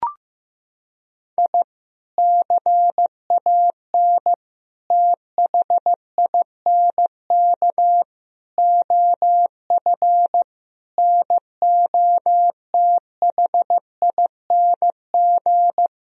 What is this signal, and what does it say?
Morse code, heard as a keyed tone, about 15 wpm, 700 Hz. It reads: I CAN THINK OF NOTHING
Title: Sentences from Top 1000 Words - Review Part 1 - 15wpm